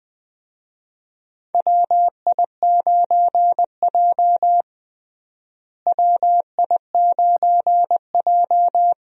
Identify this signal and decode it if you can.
Morse code, heard as a keyed tone, about 20 wpm, 700 Hz. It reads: WI9J WI9J
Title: USA Call Signs - 4 Characters - Repeated 3x - 20wpm